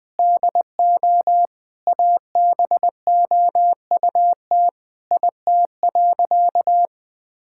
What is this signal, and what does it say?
Morse code, heard as a keyed tone, about 20 wpm, 700 Hz. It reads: DO ABOUT IT.
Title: Sentences with 8-Word Phrases - 20wpm